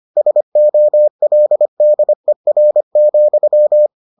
Morse code, heard as a keyed tone, about 25 wpm, 600 Hz. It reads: SOLDER,